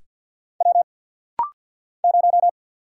Morse code, heard as a keyed tone, about 50 wpm, 700 Hz. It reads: W 0
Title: Single Letter-Number Alt - Rapid-Fire 50wpm